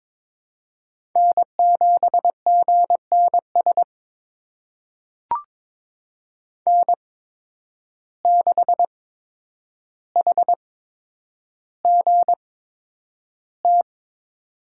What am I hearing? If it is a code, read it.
Morse code, heard as a keyed tone, about 23 wpm, spaced as for 10 wpm, 700 Hz. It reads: N7GNS N6HGT
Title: USA Call Signs - 5 Characters - 8x Character Spacing - 22wpm